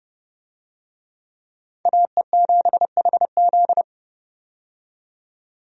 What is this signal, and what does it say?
Morse code, heard as a keyed tone, about 30 wpm, 700 Hz. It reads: AE7HZ
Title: USA Call Signs - 5 Characters - 30wpm